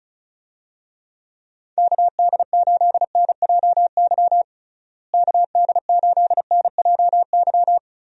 Morse code, heard as a keyed tone, about 34 wpm, 700 Hz. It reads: KD8NJY KD8NJY
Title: USA Call Signs - 6 Characters - Repeated 2x - 35wpm